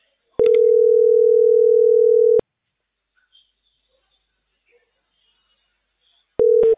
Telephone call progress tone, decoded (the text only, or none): ringback tone